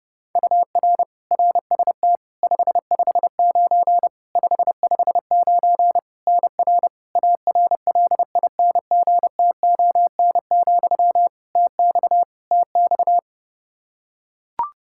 Morse code, heard as a keyed tone, about 30 wpm, 700 Hz. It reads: UR RST 559 559 NR ARLINGTON, TX TX
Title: Practice QSO - Exchange 1 - Beginning - 30wpm